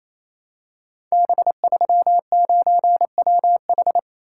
Morse code, heard as a keyed tone, about 28 wpm, 700 Hz. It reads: B39WH
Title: Code Groups - 5 Letter:Numbers - 28wpm